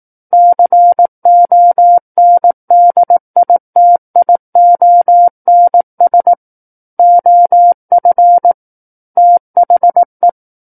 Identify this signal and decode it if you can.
Morse code, heard as a keyed tone, about 18 wpm, 700 Hz. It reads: CONDITIONS OF THE
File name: Book One / 09-Chapter 9 - 18 WPM